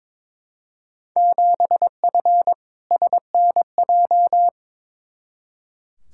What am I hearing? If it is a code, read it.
Morse code, heard as a keyed tone, about 22 wpm, 700 Hz. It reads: 7F SNJ
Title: Field Day Exchanges - 22wpm